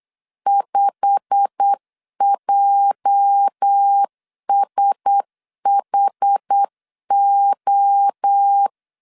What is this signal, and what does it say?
Morse code, heard as a keyed tone, about 8 wpm, 800 Hz. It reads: 5JSHO
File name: CW / DET / 1 9 1500C S A 8wpm 5637